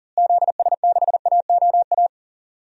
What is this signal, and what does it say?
Morse code, heard as a keyed tone, about 37 wpm, 700 Hz. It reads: ZS6AOA
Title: DXCC-100 Call Signs - 6 Characters - Repeated 3x - 40wpm